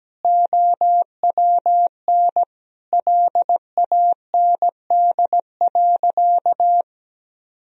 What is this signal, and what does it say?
Morse code, heard as a keyed tone, about 17 wpm, 700 Hz. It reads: OWN LAND.
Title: Sentences from Top 1000 Words - Review Part 2 - 17wpm